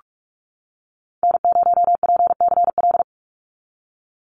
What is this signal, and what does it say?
Morse code, heard as a keyed tone, about 46 wpm, 700 Hz. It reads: N0PCL
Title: NA SOTA Activator Callsigns 45wpm